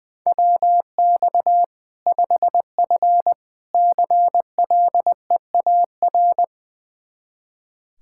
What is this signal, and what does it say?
Morse code, heard as a keyed tone, about 20 wpm, 700 Hz. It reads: WX 5F CLEAR
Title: Practice QSO - Exchange 2 - Info - 20wpm